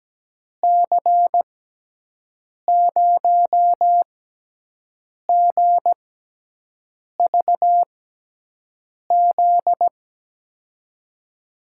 Morse code, heard as a keyed tone, about 17 wpm, spaced as for 6 wpm, 700 Hz. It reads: C0GVZ
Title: USA Call Signs - 6 Characters - 6x Character Spacing - 17wpm